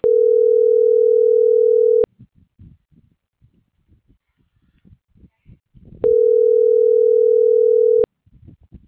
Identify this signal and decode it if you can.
Ringback tone (North America), 440 + 480 Hz.